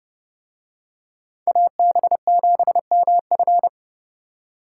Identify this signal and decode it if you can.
Morse code, heard as a keyed tone, about 28 wpm, 700 Hz. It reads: AB7MF